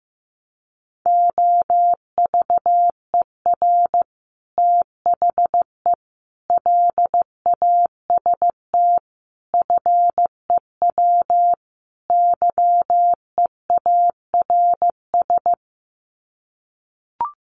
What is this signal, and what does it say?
Morse code, heard as a keyed tone, about 15 wpm, 700 Hz. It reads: OVER THE LAST FEW YEARS
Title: Sets of 5 Words Encore 15wpm